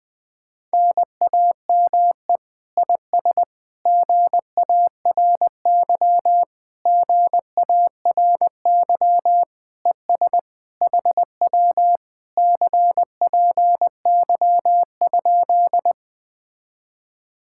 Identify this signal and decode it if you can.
Morse code, heard as a keyed tone, about 20 wpm, 700 Hz. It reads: NAME IS GARY GARY ES HW CPY?
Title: Speed-Racing Common QSO Phrases Encore - 17wpm